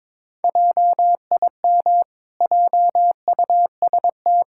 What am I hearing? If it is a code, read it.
Morse code, heard as a keyed tone, about 22 wpm, 700 Hz. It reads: JIM JUST
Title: Pangram Challenge v2 - 22wpm